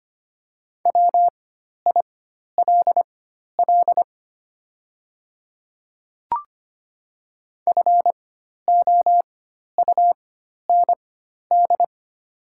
Morse code, heard as a keyed tone, about 25 wpm, spaced as for 12 wpm, 700 Hz. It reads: WILL FOUND
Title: 4x Character Spacing - Top 500 Words - 25wpm